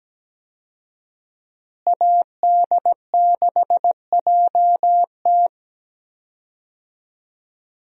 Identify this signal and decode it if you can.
Morse code, heard as a keyed tone, about 17 wpm, 700 Hz. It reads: AD6JT
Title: USA Call Signs - 5 Characters - Repeated 2x - 17wpm